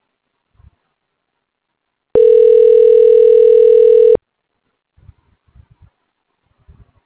Ringback tone (North America), 440 + 480 Hz.